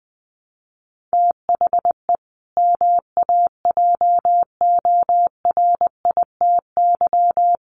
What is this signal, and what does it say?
Morse code, heard as a keyed tone, about 20 wpm, 700 Hz. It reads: THE MAJORITY
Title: Sets of 3 Words 20wpm